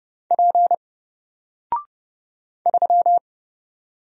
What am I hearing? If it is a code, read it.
Morse code, heard as a keyed tone, about 30 wpm, 700 Hz. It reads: P 3
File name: Single-Letter-Number-30wpm